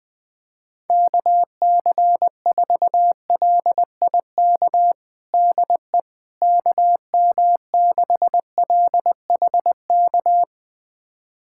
Morse code, heard as a keyed tone, about 20 wpm, 700 Hz. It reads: KC4LIK DE KM6LHK
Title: USA Callsign DE Callsign - 6 Characters - 20wpm